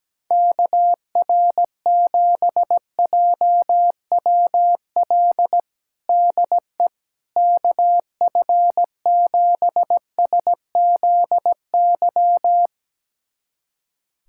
Morse code, heard as a keyed tone, about 17 wpm, 700 Hz. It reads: KR7JWL DE KF7SZY
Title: USA Callsign DE Callsign - 6 Characters - 17wpm